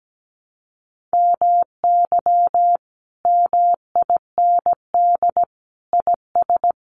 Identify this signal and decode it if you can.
Morse code, heard as a keyed tone, about 17 wpm, 700 Hz. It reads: MY MIND IS
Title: Sentences from Top 500 Words 17wpm